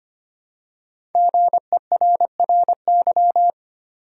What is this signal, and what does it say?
Morse code, heard as a keyed tone, about 25 wpm, 700 Hz. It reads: GERRY
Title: USA Common Names - 25wpm